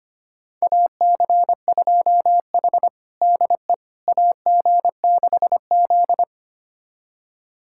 Morse code, heard as a keyed tone, about 25 wpm, 700 Hz. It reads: AC2H DE AG6Z
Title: USA Callsign DE Callsign - 4 Characters - 25wpm